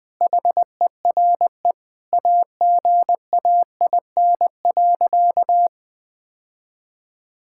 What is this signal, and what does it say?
Morse code, heard as a keyed tone, about 20 wpm, 700 Hz. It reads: HERE AGAIN.
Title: Sentences from Top 1600 Words 20wpm